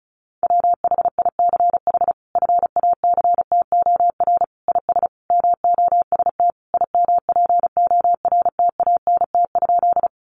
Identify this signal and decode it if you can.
Morse code, heard as a keyed tone, about 34 wpm, 700 Hz. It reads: WHICH FACTOR IS MOST IMPORTANT?
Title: Sentences from Top 1300 Words 35wpm